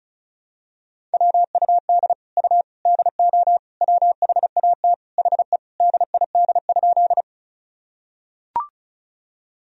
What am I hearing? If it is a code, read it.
Morse code, heard as a keyed tone, about 34 wpm, 700 Hz. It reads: WUD U DO WHAT HE DID?
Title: Sentences from Top 100 Words and Abbreviations 35wpm